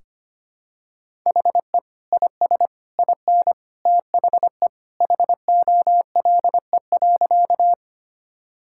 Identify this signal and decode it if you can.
Morse code, heard as a keyed tone, about 25 wpm, 700 Hz. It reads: HE IS IN THE HOLE.
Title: Sentences from Top 1600 Words 25wpm